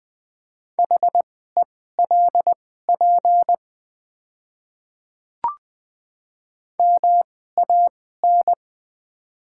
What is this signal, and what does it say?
Morse code, heard as a keyed tone, about 20 wpm, spaced as for 14 wpm, 700 Hz. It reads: HELP MAN
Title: 2x Character Spacing - Top 300 Words - 20wpm